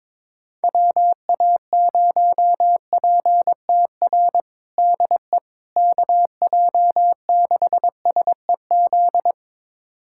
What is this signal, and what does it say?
Morse code, heard as a keyed tone, about 22 wpm, 700 Hz. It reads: WA0PTR DE KJ6SEZ
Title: USA Callsign DE Callsign - 6 Characters - 22wpm